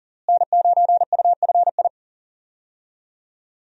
Morse code, heard as a keyed tone, about 40 wpm, 700 Hz. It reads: N9UFI
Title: USA Call Signs - 5 Characters - Repeated 3x - 40wpm